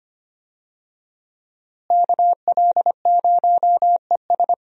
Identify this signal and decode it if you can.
Morse code, heard as a keyed tone, about 25 wpm, 700 Hz. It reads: KL0ES
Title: USA Call Signs - 5 Characters - Repeated 2x - 25wpm